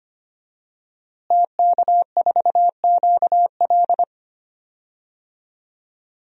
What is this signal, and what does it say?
Morse code, heard as a keyed tone, about 25 wpm, 700 Hz. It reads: TK4QL
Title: DXCC-100 Call Signs - 5 Characters - Repeated 3x - 25wpm